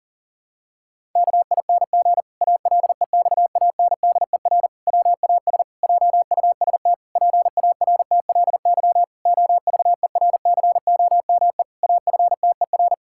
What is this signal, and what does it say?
Morse code, heard as a keyed tone, about 40 wpm, 700 Hz. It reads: KING ALEXANDER WAS JUST PARTLY OVERCOME AFTER
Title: Pangram Challenge v2 - 40wpm